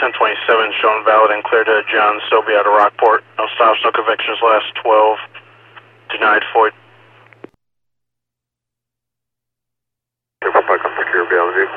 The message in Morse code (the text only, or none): none